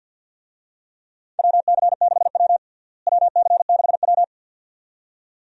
Morse code, heard as a keyed tone, about 53 wpm, 700 Hz. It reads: WC6W WC6W
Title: USA Call Signs - 4 Characters - Repeated 2x - 50wpm